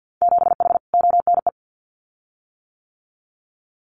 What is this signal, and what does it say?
Morse code, heard as a keyed tone, about 50 wpm, 700 Hz. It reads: 7H ONE
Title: Winter Field Day Exchanges Encore 50wpm